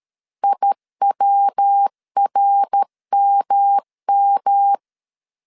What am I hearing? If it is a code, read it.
Morse code, heard as a keyed tone, about 13 wpm, 800 Hz. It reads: IWRMM
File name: CW / DET / 1 3 1500C S C 12wpm 2750